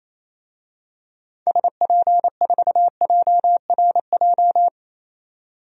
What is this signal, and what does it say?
Morse code, heard as a keyed tone, about 28 wpm, 700 Hz. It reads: SP4JRJ
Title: DXCC-100 Call Signs - 6 Characters - 28wpm